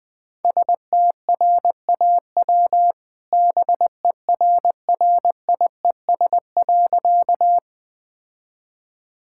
Morse code, heard as a keyed tone, about 21 wpm, 700 Hz. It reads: STRAW BERRIES.
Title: All-Age Appropriate Jokes - 20wpm